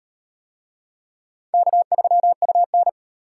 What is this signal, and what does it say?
Morse code, heard as a keyed tone, about 34 wpm, 700 Hz. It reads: K3UN